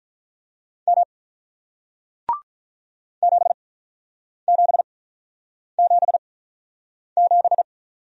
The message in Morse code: M 7777

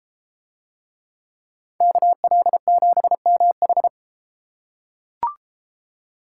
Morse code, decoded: KL7MH

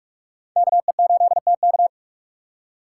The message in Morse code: KE8TK

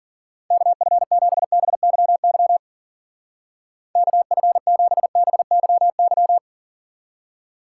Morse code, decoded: KF7BYY KF7BYY

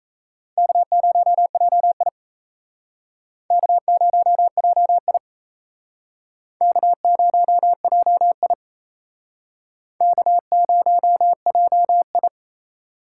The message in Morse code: K0JI K0JI K0JI K0JI

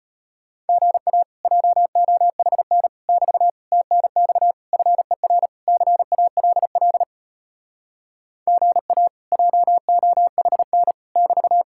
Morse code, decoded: GA JOHN <BT> TNX FER CALL GA JOHN <BT>

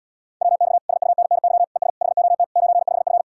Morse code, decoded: WX SUNNY ES HOT 95F

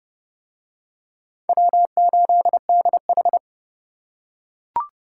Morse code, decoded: W8DH

